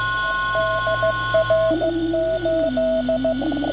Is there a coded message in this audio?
Yes: Morse code from 0.6 to 3.3 s, heard as a keyed tone, about 15 wpm, 650 Hz, with radio noise. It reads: DR8